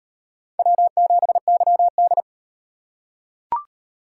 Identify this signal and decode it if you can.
Morse code, heard as a keyed tone, about 36 wpm, 700 Hz. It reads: W7YD